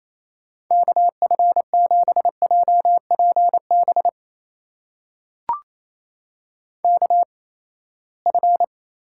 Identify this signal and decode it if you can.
Morse code, heard as a keyed tone, about 28 wpm, spaced as for 19 wpm, 700 Hz. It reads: KF7JPB KF
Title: USA Call Signs - 6 Characters - 8x Character Spacing - 28wpm